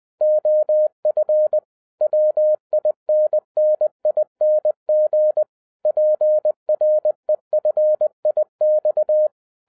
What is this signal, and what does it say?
Morse code, heard as a keyed tone, about 20 wpm, 610 Hz. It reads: OF WINNING PREFIX